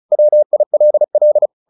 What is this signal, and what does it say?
Morse code, heard as a keyed tone, about 35 wpm, 600 Hz. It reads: WILL
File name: the old man and the sea / the old man and the sea 1 35wpm